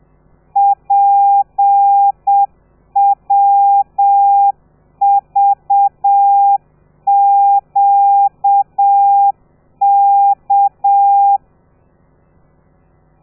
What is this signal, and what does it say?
Morse code, heard as a keyed tone, about 7 wpm, 800 Hz. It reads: PWVQK
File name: telegrafia / i2tfj / Lez13 30